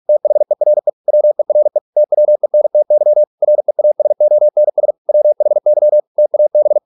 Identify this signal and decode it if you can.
Morse code, heard as a keyed tone, about 46 wpm, 600 Hz. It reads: THERE WERE TWENTY REASONS WHY TAB